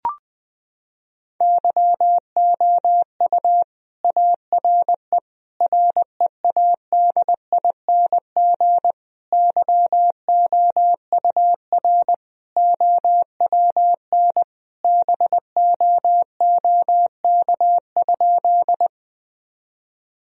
Morse code, heard as a keyed tone, about 20 wpm, 700 Hz. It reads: YOU ARE READING YOUR OWN BOOK?
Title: Sentences from Top 1000 Words - Review Part 3 - 20wpm